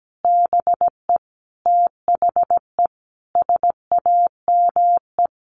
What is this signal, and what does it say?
Morse code, heard as a keyed tone, about 17 wpm, 700 Hz. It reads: BE THE SAME
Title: Sentences from Top 1300 Words 17wpm